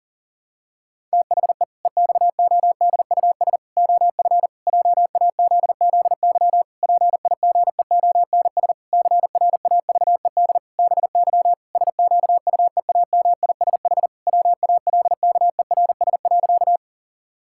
Morse code, heard as a keyed tone, about 37 wpm, 700 Hz. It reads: THE EXODUS OF JAZZY PIGEONS CRAVED BY SQUEAMISH WALKERS.
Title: Pangram Challenge v2 - 40wpm